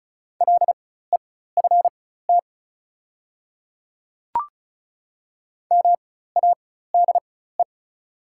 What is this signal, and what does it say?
Morse code, heard as a keyed tone, about 34 wpm, spaced as for 16 wpm, 700 Hz. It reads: LEFT MADE
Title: 4x Character Spacing - Top 300 Words - 35wpm